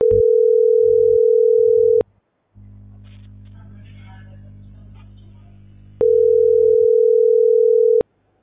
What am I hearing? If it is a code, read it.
Ringback tone (North America), 440 + 480 Hz.